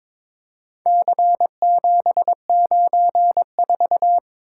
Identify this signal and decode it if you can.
Morse code, heard as a keyed tone, about 22 wpm, 700 Hz. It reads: C794
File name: practice / 22wpm / Four Letter-Number Combinations - 22wpm